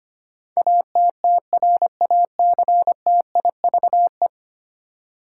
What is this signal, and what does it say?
Morse code, heard as a keyed tone, about 25 wpm, 700 Hz. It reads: ATTRACTIVE
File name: Ten Letter Words - 25wpm